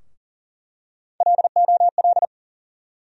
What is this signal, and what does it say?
Morse code, heard as a keyed tone, about 40 wpm, 700 Hz. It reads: LOL
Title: USA Call Sign Postfixes - 3 Letters 40wpm